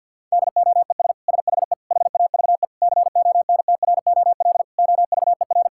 Morse code, heard as a keyed tone, about 50 wpm, 700 Hz. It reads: DOES SHE HAVE CONTROL OVER
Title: Sentences from Top 1000 Words - Review Part 5 - 50wpm